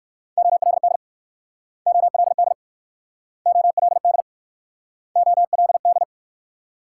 Morse code, heard as a keyed tone, about 55 wpm, 700 Hz. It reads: OLD OLD OLD OLD
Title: Speed-Racing Multi-letter Phonemes - 45wpm